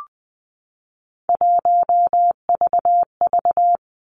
Morse code, heard as a keyed tone, about 20 wpm, 700 Hz. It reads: 1VV
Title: Three Challenging Letter-Number Combinations 20wpm